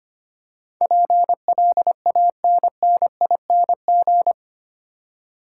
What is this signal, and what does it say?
Morse code, heard as a keyed tone, about 25 wpm, 700 Hz. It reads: PLANNING
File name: 8 Letter Words - First 3 Letters Given - 25wpm